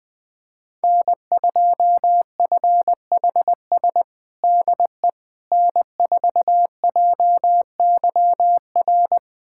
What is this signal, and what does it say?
Morse code, heard as a keyed tone, about 20 wpm, 700 Hz. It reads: N2FHS DE N4JYR
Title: USA Callsign DE Callsign - 5 Characters - 20wpm